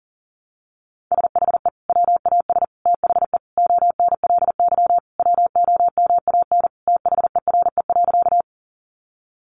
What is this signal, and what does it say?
Morse code, heard as a keyed tone, about 40 wpm, 700 Hz. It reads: SHE WAS THE ONLY WOMAN THERE.
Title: Sentences from Top 1100 Words 40wpm